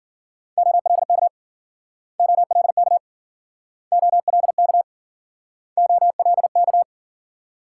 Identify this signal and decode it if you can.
Morse code, heard as a keyed tone, about 56 wpm, 700 Hz. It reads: OLK OLK OLK OLK